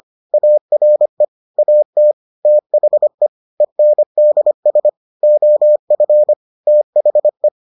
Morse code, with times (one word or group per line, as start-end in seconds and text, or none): 0.3-1.2 ARE
1.6-2.1 AT
2.5-3.3 THE
3.6-4.9 ENDS
5.2-6.3 OF
6.7-7.5 THE